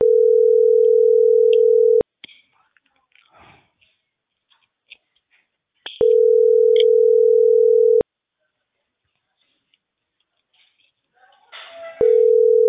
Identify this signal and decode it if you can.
Ringback tone (North America), 440 + 480 Hz.